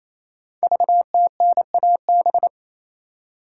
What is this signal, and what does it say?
Morse code, heard as a keyed tone, about 27 wpm, 700 Hz. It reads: VTNAB
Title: Code Groups - 5 Letters - 28wpm